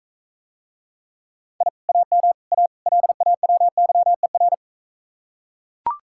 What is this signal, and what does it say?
Morse code, heard as a keyed tone, about 43 wpm, 700 Hz. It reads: I AM A LAWYER